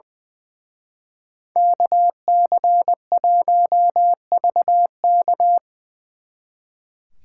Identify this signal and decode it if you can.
Morse code, heard as a keyed tone, about 23 wpm, 700 Hz. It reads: KC1VK
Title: Speed Racing - USA 2x2 Call Signs 20wpm